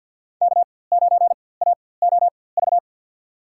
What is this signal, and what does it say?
Morse code, heard as a keyed tone, about 50 wpm, spaced as for 23 wpm, 700 Hz. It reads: K9AOV
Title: USA Call Signs - 5 Characters - 4x Character Spacing - 50wpm